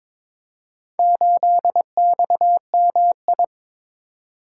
Morse code, heard as a keyed tone, about 22 wpm, 700 Hz. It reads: 8XMI
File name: practice / 22wpm / Four Letter-Number Combinations - 22wpm